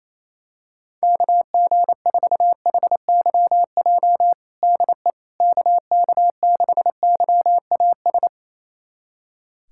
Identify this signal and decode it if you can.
Morse code, heard as a keyed tone, about 27 wpm, 700 Hz. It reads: KG4HYJ DE KK6YAS
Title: USA Callsign DE Callsign - 6 Characters - 28wpm